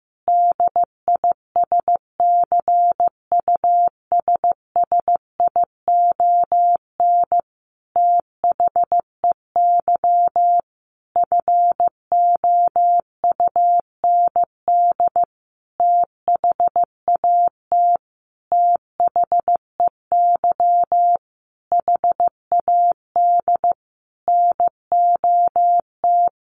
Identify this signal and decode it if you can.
Morse code, heard as a keyed tone, about 15 wpm, 700 Hz. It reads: DISCUSSION THEY FOUND THAT THEY HAD NOT
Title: Sentences from Top 1200 Words 15wpm